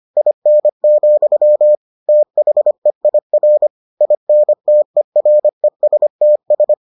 Morse code, heard as a keyed tone, about 25 wpm, 600 Hz. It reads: IN, THEIR INTERESTS